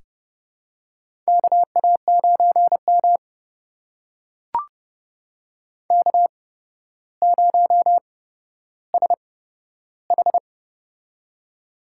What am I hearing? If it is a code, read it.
Morse code, heard as a keyed tone, about 31 wpm, spaced as for 13 wpm, 700 Hz. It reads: KA9M K0SH
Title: USA Call Signs - 4 Characters - 8x Character Spacing - 30wpm